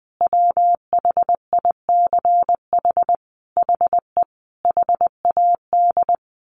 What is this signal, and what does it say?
Morse code, heard as a keyed tone, about 20 wpm, 700 Hz. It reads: WHICH HE HAD
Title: Sentences from Top 1500 Words 20wpm